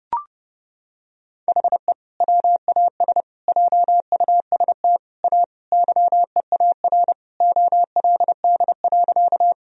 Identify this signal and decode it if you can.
Morse code, heard as a keyed tone, about 32 wpm, 700 Hz. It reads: HE WAS JUST A YEAR OLD.
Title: Sentences from Top 1000 Words - Review Part 1 - 30wpm